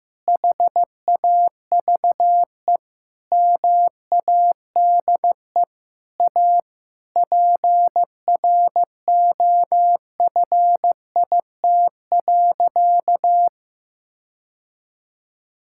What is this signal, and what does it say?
Morse code, heard as a keyed tone, about 15 wpm, 700 Hz. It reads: HAVE MADE A PROFIT.
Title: Sentences from Top 1000 Words - Review Part 5 - 15wpm